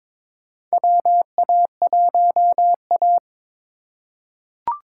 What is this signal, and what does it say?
Morse code, heard as a keyed tone, about 22 wpm, 700 Hz. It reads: WA1A
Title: USA Call Signs - 4 Characters - 2x Character Spacing - 22wpm